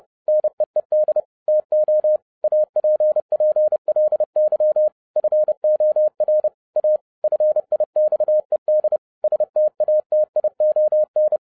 Morse code, heard as a keyed tone, about 30 wpm, 610 Hz. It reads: NEED TO APPLY FOR A FIXED STATION